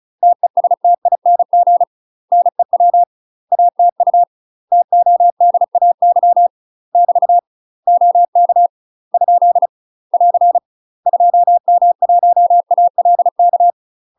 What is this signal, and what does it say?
Morse code, heard as a keyed tone, about 36 wpm, 700 Hz. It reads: TESTING NEW ATU TODAY <BT> OK ? <AR> 2M1ALK
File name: UK MASTER QSOs TRAINING / 35WPM / UK MASTER QSOs 0032